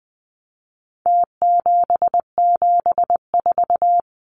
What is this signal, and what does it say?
Morse code, heard as a keyed tone, about 20 wpm, 700 Hz. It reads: T774
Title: Numbers - Contest Style - 4-Digit 20wpm